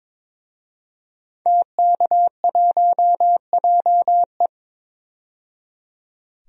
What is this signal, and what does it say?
Morse code, heard as a keyed tone, about 21 wpm, 700 Hz. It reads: TK1JE
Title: DXCC-100 Call Signs - 5 Characters - 22wpm